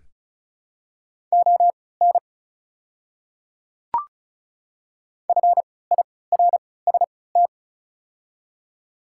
Morse code, heard as a keyed tone, about 36 wpm, spaced as for 20 wpm, 700 Hz. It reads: ON FIRST
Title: 3x Character Spacing - Top 200 Words - 35wpm